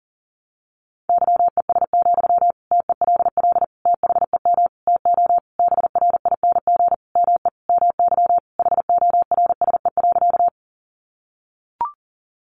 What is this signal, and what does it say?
Morse code, heard as a keyed tone, about 40 wpm, 700 Hz. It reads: YES, TELL THEM TO BRING ME MY HORSE.
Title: Sentences from Top 500 Words 40wpm